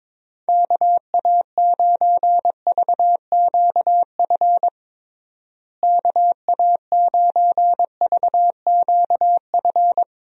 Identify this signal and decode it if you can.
Morse code, heard as a keyed tone, about 22 wpm, 700 Hz. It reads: KA9VQF KA9VQF
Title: USA Call Signs - 6 Characters - Repeated 2x - 22wpm